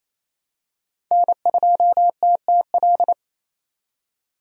Morse code, heard as a keyed tone, about 28 wpm, 700 Hz. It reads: N2TTL
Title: USA Call Signs - 5 Characters - 4x Character Spacing - 28wpm